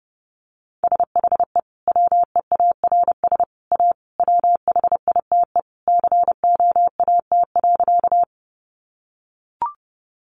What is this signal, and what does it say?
Morse code, heard as a keyed tone, about 30 wpm, 700 Hz. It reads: SHE WEARS A WHITE COAT.
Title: Additional Easy Phrases 30wpm